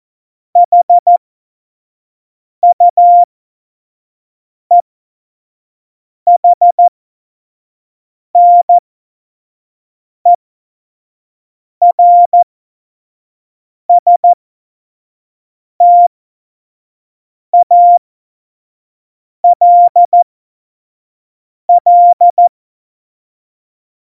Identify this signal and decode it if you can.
Morse code, heard as a keyed tone, about 14 wpm, spaced as for 5 wpm, 700 Hz. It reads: HUEHNERSTALL
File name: Morsekurs / Witze / Witz 23. Oktober 2018, 25 F70